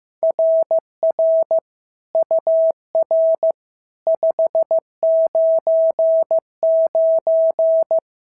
Morse code, heard as a keyed tone, about 14 wpm, 650 Hz. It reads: RR UR 599
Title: SOTA QSO Practice 15wpm